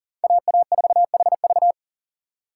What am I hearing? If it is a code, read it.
Morse code, heard as a keyed tone, about 40 wpm, 700 Hz. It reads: AA4HV
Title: K1USN SST QSO Practice - 40wpm